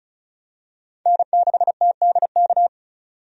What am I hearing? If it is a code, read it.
Morse code, heard as a keyed tone, about 34 wpm, 700 Hz. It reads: N6TDK
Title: USA Call Signs - 5 Characters - 35wpm